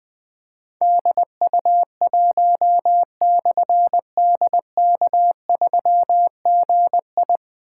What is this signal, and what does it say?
Morse code, heard as a keyed tone, about 20 wpm, 700 Hz. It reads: DU1/DK3GI
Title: Difficult Call Signs 20wpm